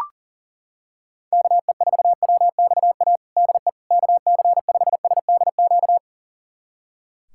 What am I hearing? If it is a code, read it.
Morse code, heard as a keyed tone, about 40 wpm, 700 Hz. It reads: KE4WXA DE KC5SDQ
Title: USA Callsign DE Callsign - 6 Characters - 40wpm